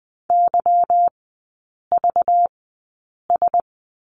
Morse code, heard as a keyed tone, about 20 wpm, spaced as for 8 wpm, 700 Hz. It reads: YVS
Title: 191 Lesson 31e - Letter v - Three Characters 20wpm